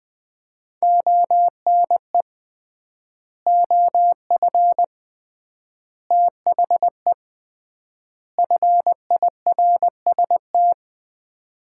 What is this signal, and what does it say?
Morse code, heard as a keyed tone, about 20 wpm, 700 Hz. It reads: ONE OF THE FIRST
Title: 3x Word-Spacing - Sets of 4 Words - 20wpm